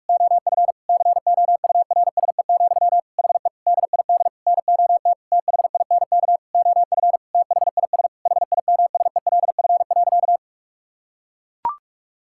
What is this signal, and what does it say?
Morse code, heard as a keyed tone, about 41 wpm, 700 Hz. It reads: OF COURSE, HE DID NOT THINK OF THIS HIMSELF.
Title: Sentences from Top 1600 Words 45wpm